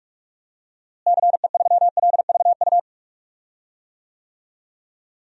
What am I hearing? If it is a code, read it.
Morse code, heard as a keyed tone, about 46 wpm, 700 Hz. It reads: CE3LVU
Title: DXCC-100 Call Signs - 6 Characters - Repeated 2x - 45wpm